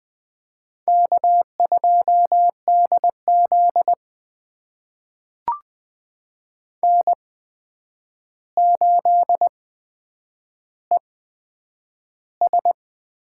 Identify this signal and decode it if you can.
Morse code, heard as a keyed tone, about 20 wpm, spaced as for 9 wpm, 700 Hz. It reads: K2DZ N8ES
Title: USA Call Signs - 4 Characters - 8x Character Spacing - 20wpm